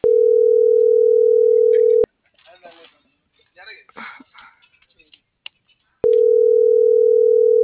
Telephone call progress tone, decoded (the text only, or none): ringback tone